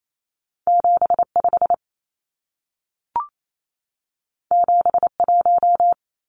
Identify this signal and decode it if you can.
Morse code, heard as a keyed tone, about 28 wpm, 700 Hz. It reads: 75 71
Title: Numbers - 2-Digit 28wpm